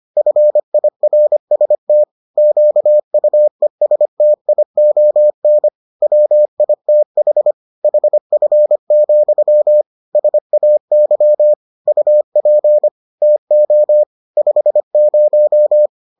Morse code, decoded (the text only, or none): FIRST QUESTION WITH HF, SAY UP TO 50